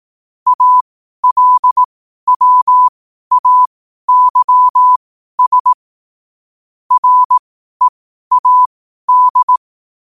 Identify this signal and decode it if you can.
Morse code, heard as a keyed tone, about 18 wpm, spaced as for 12 wpm, 1000 Hz. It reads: ALWAYS READ
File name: qotdfiles / 12wpm